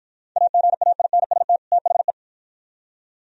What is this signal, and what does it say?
Morse code, heard as a keyed tone, about 54 wpm, 700 Hz. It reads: AGAINST THE